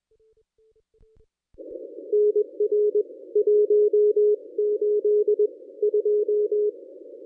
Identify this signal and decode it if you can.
Morse code, heard as a keyed tone, about 24 wpm, spaced as for 19 wpm, 410 Hz. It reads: NR 182
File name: contests / ss-cw-k5tr-op-n5rz -1000